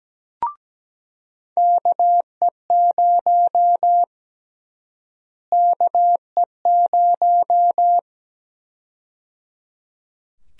Morse code, heard as a keyed tone, about 17 wpm, 700 Hz. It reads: KE0 KE0
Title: USA Call Sign Prefixes - Repeated 2x - 17wpm